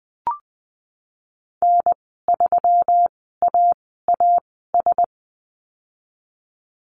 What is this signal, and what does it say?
Morse code, heard as a keyed tone, about 19 wpm, spaced as for 14 wpm, 700 Hz. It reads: N3AAS
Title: 140 Lesson 23g - Letter f - Call Signs 20wpm